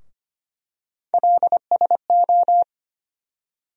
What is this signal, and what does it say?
Morse code, heard as a keyed tone, about 25 wpm, 700 Hz. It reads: LSO
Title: USA Call Sign Postfixes - 3 Letters 25wpm